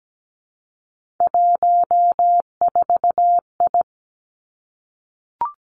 Morse code, heard as a keyed tone, about 17 wpm, 700 Hz. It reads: 14I
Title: Three Challenging Letter-Number Combinations 17wpm